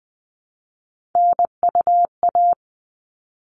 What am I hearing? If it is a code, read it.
Morse code, heard as a keyed tone, about 20 wpm, 700 Hz. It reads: NUA
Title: All Cut Numbers - 3-Digit 20wpm